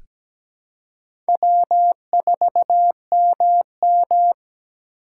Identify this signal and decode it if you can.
Morse code, heard as a keyed tone, about 17 wpm, 700 Hz. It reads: W4MM
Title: USA Call Signs - 4 Characters - 3x Character Spacing - 17wpm